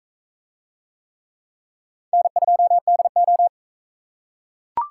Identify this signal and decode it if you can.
Morse code, heard as a keyed tone, about 38 wpm, 700 Hz. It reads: N2DO